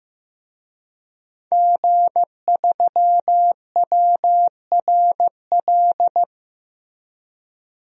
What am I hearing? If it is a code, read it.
Morse code, heard as a keyed tone, about 15 wpm, 700 Hz. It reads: G3WRL
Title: DXCC-100 Call Signs - 5 Characters - Repeated 2x - 15wpm